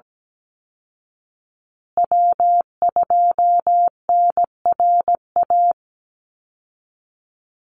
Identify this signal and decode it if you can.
Morse code, heard as a keyed tone, about 17 wpm, 700 Hz. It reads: W2NRA
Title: Call Signs Repeated 2x - 17wpm